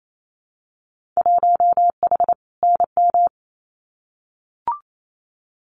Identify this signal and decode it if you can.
Morse code, heard as a keyed tone, about 28 wpm, 700 Hz. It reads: 1H NM
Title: Winter Field Day Exchanges 28wpm